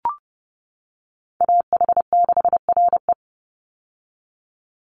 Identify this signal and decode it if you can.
Morse code, heard as a keyed tone, about 30 wpm, 700 Hz. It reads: AH6RE
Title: Call Signs - 30wpm